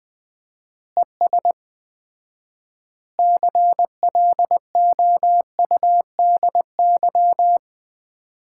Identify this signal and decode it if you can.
Morse code, heard as a keyed tone, about 21 wpm, 700 Hz. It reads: ES CLOUDY
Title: Common QSO Phrases - 4x Word-Spacing - 20wpm